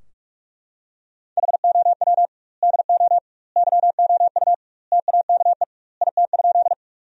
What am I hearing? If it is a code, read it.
Morse code, heard as a keyed tone, about 43 wpm, 700 Hz. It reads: HOW DO YOU TAKE IT?
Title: 2x Word-Spacing - Sentences from Top 200 Words - 45wpm